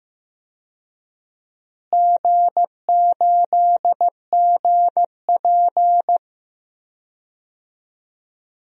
Morse code, heard as a keyed tone, about 15 wpm, 700 Hz. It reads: G8GP
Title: DXCC-100 Call Signs - 4 Characters - Repeated 2x - 15wpm